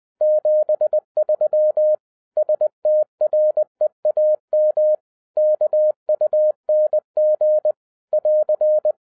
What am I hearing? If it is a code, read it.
Morse code, heard as a keyed tone, about 20 wpm, 610 Hz. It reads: 73 STREAM KUNG <AR>
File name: 2011-07-mp3 / n03 KB0IZA 20wpm